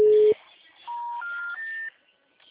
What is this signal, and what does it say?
Special information tone.
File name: out- -2009